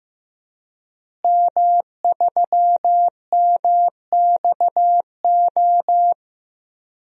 Morse code, heard as a keyed tone, about 15 wpm, 700 Hz. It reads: M3MXO
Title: DXCC-100 Call Signs - 5 Characters - Repeated 2x - 15wpm